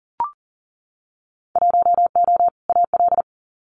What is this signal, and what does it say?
Morse code, heard as a keyed tone, about 38 wpm, 700 Hz. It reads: 1O AL
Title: Winter Field Day Exchanges 40wpm